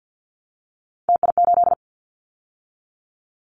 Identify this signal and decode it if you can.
Morse code, heard as a keyed tone, about 48 wpm, 700 Hz. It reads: TI8
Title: Callsign Prefixes for Top 100 DXCC - 50wpm